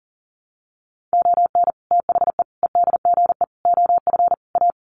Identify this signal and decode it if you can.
Morse code, heard as a keyed tone, about 40 wpm, 700 Hz. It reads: ON THE EDGE OF A
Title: Sets of 5 Words Encore2 40wpm